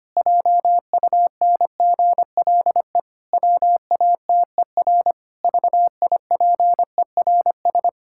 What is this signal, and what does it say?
Morse code, heard as a keyed tone, about 25 wpm, 700 Hz. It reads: JUNGLE WATER VIPERS
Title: Pangram Challenge v2 - 25wpm